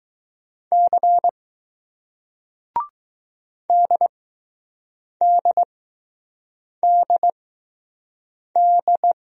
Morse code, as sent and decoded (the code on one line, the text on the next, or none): -.-. / -.. -.. -.. -..
C DDDD